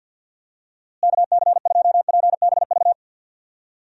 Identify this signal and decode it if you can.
Morse code, heard as a keyed tone, about 52 wpm, 700 Hz. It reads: KC2PBV
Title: USA Call Signs - 6 Characters - Repeated 3x - 50wpm